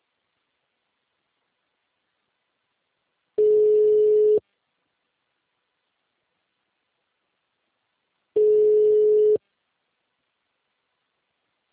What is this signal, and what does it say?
Ringback tone (Europe, 425 Hz).